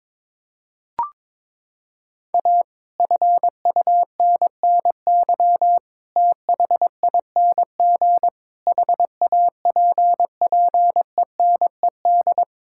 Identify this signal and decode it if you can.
Morse code, heard as a keyed tone, about 21 wpm, 700 Hz. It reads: A FUNNY THING HAPPENED
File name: practice / 22wpm / Sets of 8 Words - 22wpm